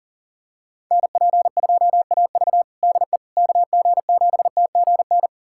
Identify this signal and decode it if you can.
Morse code, heard as a keyed tone, about 40 wpm, 700 Hz. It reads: NP2AV DE KG7TGN
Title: USA Callsign DE Callsign 40wpm